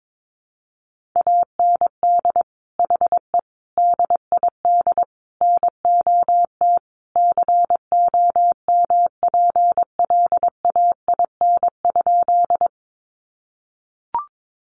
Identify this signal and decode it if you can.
Morse code, heard as a keyed tone, about 22 wpm, 700 Hz. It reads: AND HE DID NOT COMPLAIN?
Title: Sentences from Top 1300 Words 22wpm